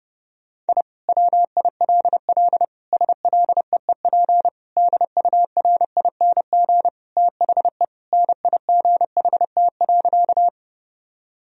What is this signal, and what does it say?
Morse code, heard as a keyed tone, about 30 wpm, 700 Hz. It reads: I WILL SLEEP DURING THE NIGHT.
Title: Sentences from Top 1000 Words - Review Part 2 - 30wpm